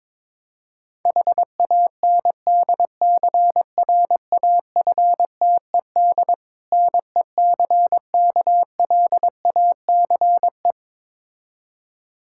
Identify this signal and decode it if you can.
Morse code, heard as a keyed tone, about 22 wpm, 700 Hz. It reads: HANDCRAFTED NECKLACE
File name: Difficult Words with Hints - 22wpm